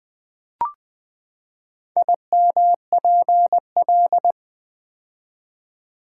Morse code, heard as a keyed tone, about 20 wpm, 700 Hz. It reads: IMPL
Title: Build A Word - 4 Letters At A Time - 11-Letter Words - 20wpm